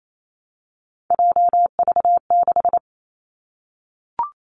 Morse code, heard as a keyed tone, about 28 wpm, 700 Hz. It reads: JV6
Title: Three Challenging Letter-Number Combinations 28wpm